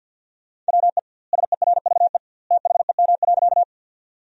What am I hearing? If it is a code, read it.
Morse code, heard as a keyed tone, about 50 wpm, 700 Hz. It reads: WE SERVE THEM.
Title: 2x Word-Spacing - Sentences from Top 400 Words - 50wpm